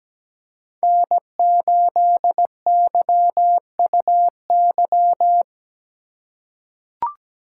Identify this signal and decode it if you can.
Morse code, heard as a keyed tone, about 17 wpm, 700 Hz. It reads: N8YUY
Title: USA Call Signs - 4x Character Spacing - 17wpm